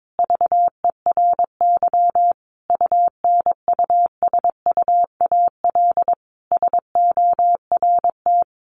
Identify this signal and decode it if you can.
Morse code, heard as a keyed tone, about 22 wpm, 700 Hz. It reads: VERY UNUSUAL SORT
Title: Sentences from Top 1400 Words 22wpm